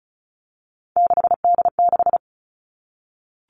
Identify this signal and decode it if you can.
Morse code, heard as a keyed tone, about 34 wpm, 700 Hz. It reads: 6D6
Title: All Cut Numbers - 3-Digit 35wpm